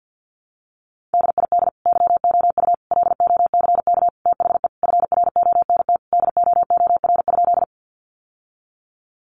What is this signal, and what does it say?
Morse code, heard as a keyed tone, about 50 wpm, 700 Hz. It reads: DID YOU LOCK THE FRONT DOOR?
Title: Sentences from Top 1300 Words 50wpm